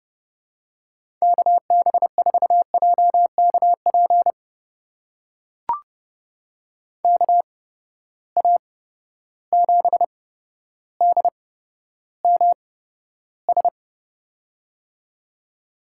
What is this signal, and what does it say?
Morse code, heard as a keyed tone, about 30 wpm, spaced as for 13 wpm, 700 Hz. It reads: KB4JKP KA7DMS
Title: USA Call Signs - 6 Characters - 8x Character Spacing - 30wpm